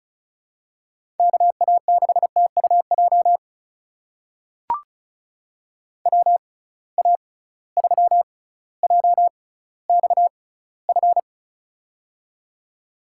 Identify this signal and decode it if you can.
Morse code, heard as a keyed tone, about 36 wpm, spaced as for 18 wpm, 700 Hz. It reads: KA6TUJ WA3JXF
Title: USA Call Signs - 6 Characters - 6x Character Spacing - 35wpm